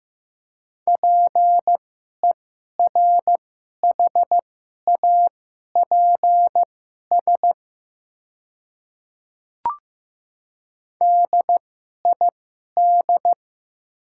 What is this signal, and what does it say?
Morse code, heard as a keyed tone, about 15 wpm, spaced as for 11 wpm, 700 Hz. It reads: PERHAPS DID
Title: 2x Character Spacing - Top 500 Words - 15wpm